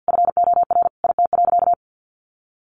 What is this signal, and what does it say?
Morse code, heard as a keyed tone, about 50 wpm, 700 Hz. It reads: FOR IT.
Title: Sentences from Top 1300 Words 50wpm